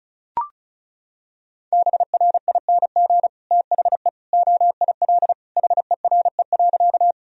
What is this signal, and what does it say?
Morse code, heard as a keyed tone, about 34 wpm, 700 Hz. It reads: BRING THE OIL HERE.
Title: Sentences from Top 1000 Words - Review Part 4 - 35wpm